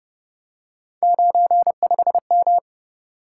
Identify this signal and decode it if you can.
Morse code, heard as a keyed tone, about 31 wpm, 700 Hz. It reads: 95M
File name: Speed-Racing - Three Letter-Number Combinations - 25wpm